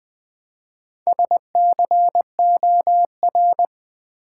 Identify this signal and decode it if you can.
Morse code, heard as a keyed tone, about 20 wpm, 700 Hz. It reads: SCOR
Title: Build A Word - 1 Letter At A Time - 3 to 5-Letter Words - 20wpm